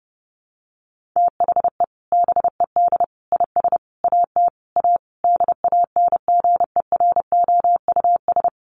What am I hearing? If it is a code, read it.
Morse code, heard as a keyed tone, about 30 wpm, 700 Hz. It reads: THE BED IS AT A DANGEROUS
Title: Sentences from Top 1500 Words 30wpm